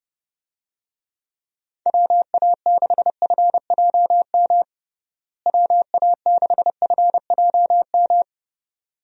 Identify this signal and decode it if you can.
Morse code, heard as a keyed tone, about 30 wpm, 700 Hz. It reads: WA6FJM WA6FJM
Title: USA Call Signs - 6 Characters - Repeated 2x - 30wpm